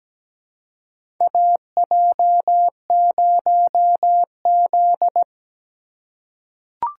Morse code, heard as a keyed tone, about 17 wpm, 700 Hz. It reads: AJ0Z
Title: USA Call Signs - 4 Characters - 2x Character Spacing - 17wpm